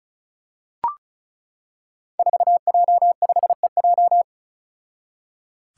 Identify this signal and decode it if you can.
Morse code, heard as a keyed tone, about 34 wpm, 700 Hz. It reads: 4J5EJ
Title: Five Challenging Letter-Number Combinations 35wpm